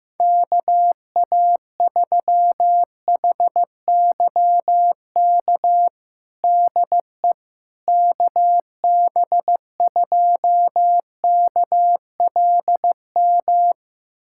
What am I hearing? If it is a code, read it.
Morse code, heard as a keyed tone, about 15 wpm, 700 Hz. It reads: KA3HYK DE KB2KLM
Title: USA Callsign DE Callsign - 6 Characters - 15wpm